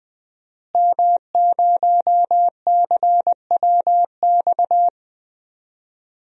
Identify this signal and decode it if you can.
Morse code, heard as a keyed tone, about 20 wpm, 700 Hz. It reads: M0CWX
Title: DXCC-100 Call Signs - 5 Characters - Repeated 3x - 20wpm